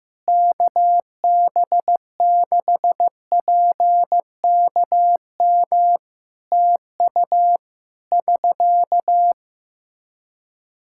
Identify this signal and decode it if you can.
Morse code, heard as a keyed tone, about 15 wpm, 700 Hz. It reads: KB6PKM TU <SK>
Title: Practice QSO - Exchange 4 - Ending - 15wpm